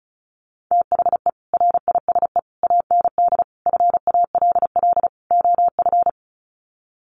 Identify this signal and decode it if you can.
Morse code, heard as a keyed tone, about 34 wpm, 700 Hz. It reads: THE RISE AND FALL OF
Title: Sets of 5 Words Encore2 35wpm